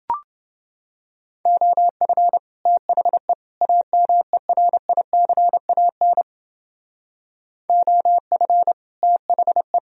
Morse code, heard as a keyed tone, about 29 wpm, 700 Hz. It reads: OF THE AMERICAN OF THE